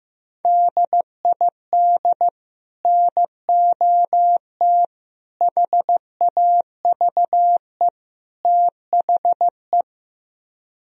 Morse code, heard as a keyed tone, about 15 wpm, 700 Hz. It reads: DID NOT HAVE THE
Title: Speed-Racing - Sets of 4 Words 15wpm